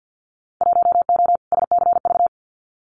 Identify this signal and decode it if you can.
Morse code, heard as a keyed tone, about 50 wpm, 700 Hz. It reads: 1O SCV
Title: Winter Field Day Exchanges 50wpm